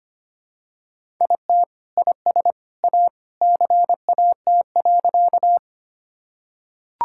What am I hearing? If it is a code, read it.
Morse code, heard as a keyed tone, about 25 wpm, 700 Hz. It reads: IT IS A CAT.
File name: practice / 25wpm / Sentences with 2-Word Phrases - 25wpm